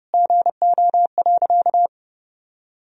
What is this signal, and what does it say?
Morse code, heard as a keyed tone, about 32 wpm, 700 Hz. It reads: GO.
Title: Sentences from Top 1000 Words - Review Part 1 - 30wpm